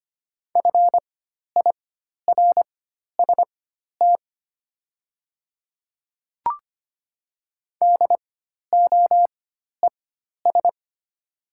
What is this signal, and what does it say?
Morse code, heard as a keyed tone, about 25 wpm, spaced as for 12 wpm, 700 Hz. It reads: FIRST DOES
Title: 4x Character Spacing - Top 300 Words - 25wpm